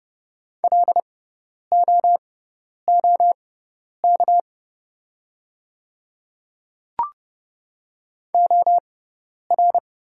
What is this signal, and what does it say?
Morse code, heard as a keyed tone, about 30 wpm, spaced as for 10 wpm, 690 Hz. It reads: LOOK OR
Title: 6x Character Spacing - Top 100 Words - 30wpm